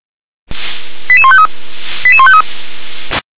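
Five-tone selective call (ZVEI), heard as quick runs of five tones, tones of about 70 ms. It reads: 88153 88153
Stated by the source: ZVEI five-tone selective call received on an SDR.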